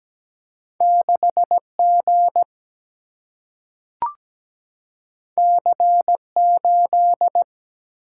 Morse code, heard as a keyed tone, about 17 wpm, 700 Hz. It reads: 6 G C 8
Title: Single-Letter Single-Number 17wpm